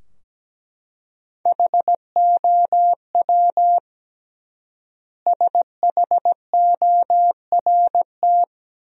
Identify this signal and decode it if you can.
Morse code, heard as a keyed tone, about 17 wpm, 700 Hz. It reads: HOW SHORT
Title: 3x Word-Spacing - Sentences from Top 400 Words - 17wpm